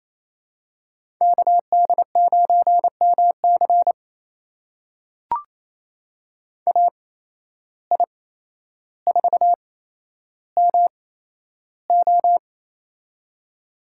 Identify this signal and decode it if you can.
Morse code, heard as a keyed tone, about 28 wpm, spaced as for 12 wpm, 700 Hz. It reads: KD9MC AI4MO
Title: USA Call Signs - 5 Characters - 8x Character Spacing - 28wpm